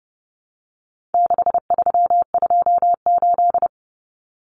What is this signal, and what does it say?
Morse code, heard as a keyed tone, about 30 wpm, 700 Hz. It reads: 6328
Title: Common Cut Numbers - 4-Digit 30wpm